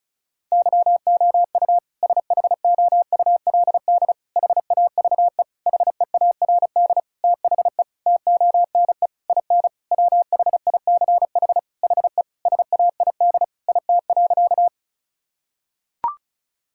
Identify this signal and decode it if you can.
Morse code, heard as a keyed tone, about 34 wpm, 700 Hz. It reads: YOU SHOULD HAVE HEARD THE TONE IN WHICH HE SAID IT.
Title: Sentences from Top 1600 Words 35wpm